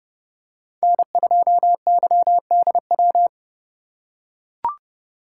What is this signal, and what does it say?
Morse code, heard as a keyed tone, about 30 wpm, 700 Hz. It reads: N2YDW